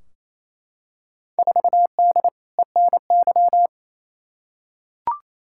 Morse code, heard as a keyed tone, about 28 wpm, 700 Hz. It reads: 4D ENY
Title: Field Day Exchanges Encore - 28wpm